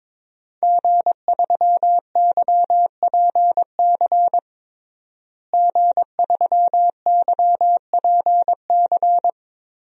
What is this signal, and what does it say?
Morse code, heard as a keyed tone, about 22 wpm, 700 Hz. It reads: G3YPC G3YPC
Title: DXCC-100 Call Signs - 5 Characters - Repeated 2x - 22wpm